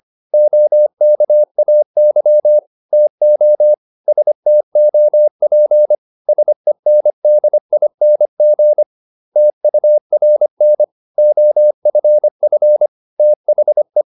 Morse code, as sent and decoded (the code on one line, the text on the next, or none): --- -.- .- -.-- / - --- / ... - --- .--. / ... -. -.. .. -. --. / - ..- .-. -. / --- ..-. ..-. / - .... .
OKAY TO STOP SENDING TURN OFF THE